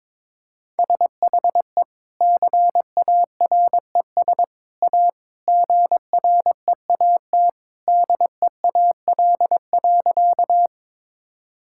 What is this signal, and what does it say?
Morse code, heard as a keyed tone, about 22 wpm, 700 Hz. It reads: SHE CARES A GREAT DEAL.
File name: practice / 22wpm / Sentences with 3-Word Phrases - 22wpm